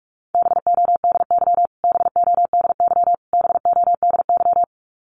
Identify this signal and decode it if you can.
Morse code, heard as a keyed tone, about 44 wpm, 700 Hz. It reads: BODY BODY BODY
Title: Top 500 Words Repeated 45wpm